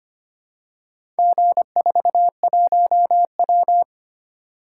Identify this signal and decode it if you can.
Morse code, heard as a keyed tone, about 25 wpm, 700 Hz. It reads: G41W
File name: practice / 25wpm / Four Letter-Number Combinations - 25wpm